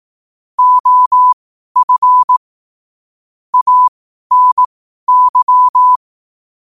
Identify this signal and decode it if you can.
Morse code, heard as a keyed tone, about 18 wpm, spaced as for 12 wpm, 1000 Hz. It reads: OF ANY
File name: qotdfiles / 12wpm